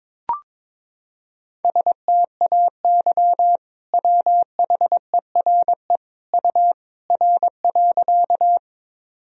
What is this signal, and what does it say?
Morse code, heard as a keyed tone, about 22 wpm, 700 Hz. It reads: STAY WHERE U R.
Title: Sentences from Top 500 Words and Abbreviations 22wpm